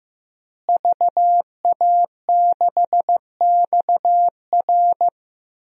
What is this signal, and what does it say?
Morse code, heard as a keyed tone, about 15 wpm, 700 Hz. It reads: VA6XR
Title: DXCC-100 Call Signs - 5 Characters - Repeated 3x - 15wpm